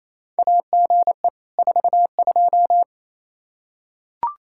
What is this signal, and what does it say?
Morse code, heard as a keyed tone, about 28 wpm, 700 Hz. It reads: AGE 42
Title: Practice QSO - Exchange 3 - More Info - 28wpm